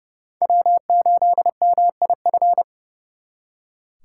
Morse code, heard as a keyed tone, about 39 wpm, 700 Hz. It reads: W8MIF